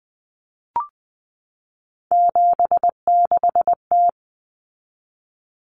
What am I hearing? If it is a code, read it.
Morse code, heard as a keyed tone, about 20 wpm, 700 Hz. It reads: 76T
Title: Common Cut Numbers - 3-Digit 20wpm